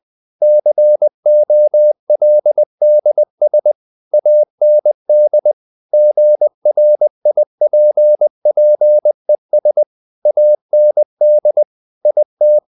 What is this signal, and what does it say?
Morse code, heard as a keyed tone, about 20 wpm, 600 Hz. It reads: COLDS AND GRIPPES AND IT
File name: the old man and the sea / the old man and the sea part 1 20wpm